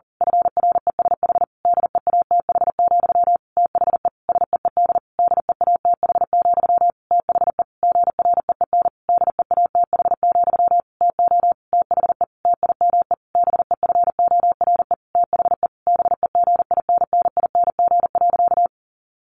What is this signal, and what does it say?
Morse code, heard as a keyed tone, about 40 wpm, 700 Hz. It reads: FRESH DEATH, THE SEED DEATH, THE GREEN DEATH, TO THE TIME BEFORE THE BEGINNING.
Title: A Sound Of Thunder 40wpm